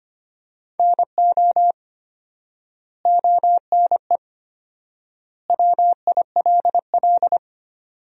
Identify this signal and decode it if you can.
Morse code, heard as a keyed tone, about 25 wpm, 700 Hz. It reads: NO ONE WILL
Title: 4x Word-Spacing - Sentences from Top 100 Words - 25wpm